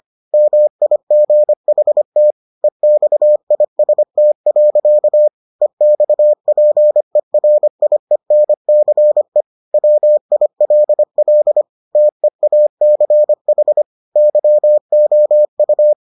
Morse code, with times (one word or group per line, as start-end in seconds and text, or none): 0.3-2.3 MIGHT
2.6-5.3 EXIST.
5.6-9.4 EXPERIENCE
9.7-11.6 WILL
12.0-13.8 TEACH
14.2-15.9 YOU